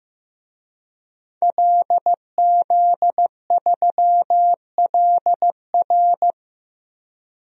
Morse code, heard as a keyed tone, about 15 wpm, 700 Hz. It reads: LZ3LR
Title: DXCC-100 Call Signs - 5 Characters - 15wpm